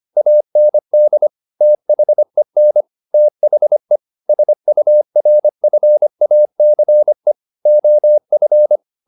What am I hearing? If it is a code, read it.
Morse code, heard as a keyed tone, about 25 wpm, 600 Hz. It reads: AND THEN THE SURFACE OF
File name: the old man and the sea part 1 25wpm